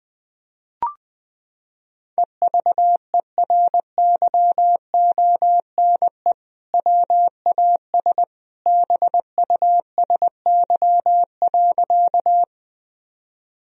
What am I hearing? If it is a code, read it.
Morse code, heard as a keyed tone, about 20 wpm, 700 Hz. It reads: EVERYONE WAS BUSY.
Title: Sentences from Top 1000 Words - Review Part 3 - 20wpm